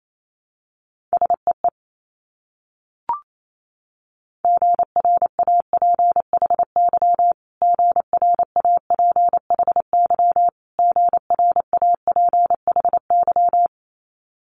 Morse code, heard as a keyed tone, about 28 wpm, 700 Hz. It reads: SEE GRAPHY GRAPHY GRAPHY
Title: Multi-letter Phonemes Repeated 3x 28wpm